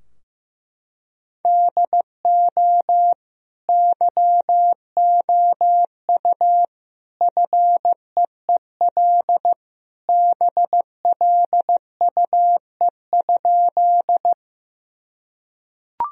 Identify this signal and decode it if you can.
Morse code, heard as a keyed tone, about 15 wpm, 700 Hz. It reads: DO YOU FEEL BLUE?
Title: Sentences from Top 1000 Words - Review Part 3 - 15wpm